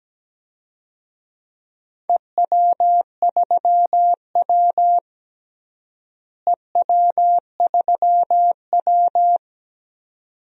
Morse code, heard as a keyed tone, about 17 wpm, 700 Hz. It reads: EW3W EW3W
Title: DXCC-100 Call Signs - 4 Characters - Repeated 2x - 17wpm